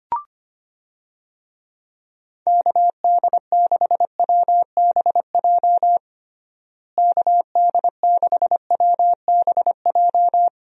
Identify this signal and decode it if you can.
Morse code, heard as a keyed tone, about 25 wpm, 700 Hz. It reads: KD6WBJ KD6WBJ
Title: USA Call Signs - 6 Characters - Repeated 3x - 25wpm